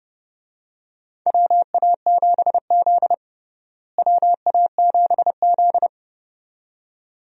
Morse code, heard as a keyed tone, about 32 wpm, 700 Hz. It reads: WA7Z WA7Z
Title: USA Call Signs - 4 Characters - Repeated 2x - 30wpm